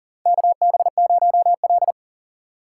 Morse code, heard as a keyed tone, about 40 wpm, 700 Hz. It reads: KB0L
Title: USA Call Signs - 4 Characters - 8x Character Spacing - 40wpm